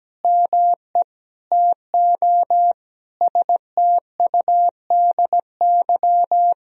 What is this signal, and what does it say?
Morse code, heard as a keyed tone, about 17 wpm, 700 Hz. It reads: ME TO STUDY
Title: Sentences from Top 1000 Words - Review Part 3 - 17wpm